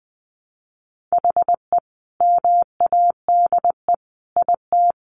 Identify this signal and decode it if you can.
Morse code, heard as a keyed tone, about 19 wpm, 700 Hz. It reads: HE MADE IT
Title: Sentences from Top 1400 Words 20wpm